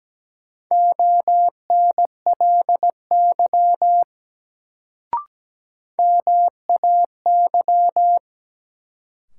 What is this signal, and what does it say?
Morse code, heard as a keyed tone, about 17 wpm, 700 Hz. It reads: ONLY MAY
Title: Sets of 2 Words - New Word Drill - 17wpm